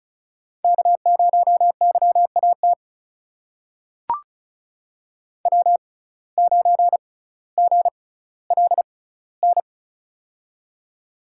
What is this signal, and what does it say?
Morse code, heard as a keyed tone, about 34 wpm, spaced as for 18 wpm, 700 Hz. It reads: K0YAT W9GLN
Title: USA Call Signs - 5 Characters - 6x Character Spacing - 35wpm